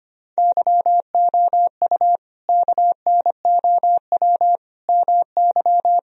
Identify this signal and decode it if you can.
Morse code, heard as a keyed tone, about 25 wpm, 700 Hz. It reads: YOU KNOW MY
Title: Sentences from Top 1000 Words - Review Part 3 - 25wpm